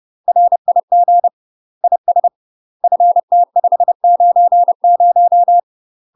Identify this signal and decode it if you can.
Morse code, heard as a keyed tone, about 30 wpm, 700 Hz. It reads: RIG IS FT590
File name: UK MASTER QSOs TRAINING / 30WPM / UK MASTER QSOs 0002